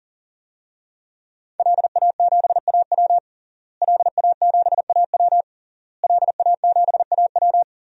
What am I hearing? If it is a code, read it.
Morse code, heard as a keyed tone, about 40 wpm, 700 Hz. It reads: LA7AW LA7AW LA7AW
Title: DXCC-100 Call Signs - 5 Characters - Repeated 3x - 40wpm